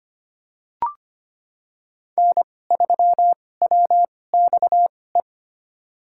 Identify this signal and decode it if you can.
Morse code, heard as a keyed tone, about 25 wpm, spaced as for 18 wpm, 700 Hz. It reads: N3WXE
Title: USA Call Signs - 2x Character Spacing - 25wpm